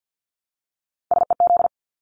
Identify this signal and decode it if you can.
Morse code, heard as a keyed tone, about 49 wpm, 700 Hz. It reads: SEZ
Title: Abbreviations 50wpm